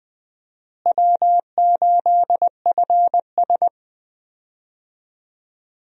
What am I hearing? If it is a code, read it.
Morse code, heard as a keyed tone, about 20 wpm, 700 Hz. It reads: W8FS
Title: USA Call Signs - 4 Characters - Repeated 3x - 20wpm